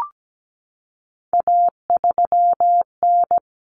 Morse code, heard as a keyed tone, about 17 wpm, 700 Hz. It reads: A3N
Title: Common Cut Numbers - 3-Digit 17wpm